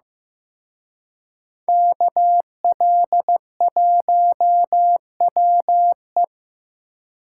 Morse code, heard as a keyed tone, about 18 wpm, 700 Hz. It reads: KL1WE
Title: Speed Racing - USA 2x2 Call Signs 15wpm